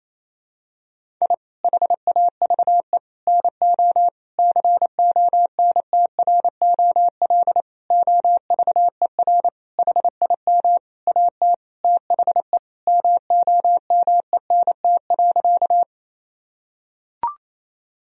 Morse code, heard as a keyed tone, about 28 wpm, 700 Hz. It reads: I HAVE NO CONTROL OVER HIM AT THE MOMENT.
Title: Sentences from Top 1000 Words - Review Part 3 - 28wpm